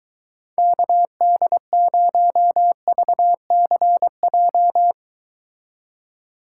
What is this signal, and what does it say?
Morse code, heard as a keyed tone, about 22 wpm, 700 Hz. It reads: KD0VCJ